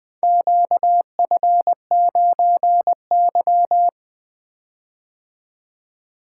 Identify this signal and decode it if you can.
Morse code, heard as a keyed tone, about 20 wpm, 700 Hz. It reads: QF9Y
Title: Four Letter-Number Combinations - 20wpm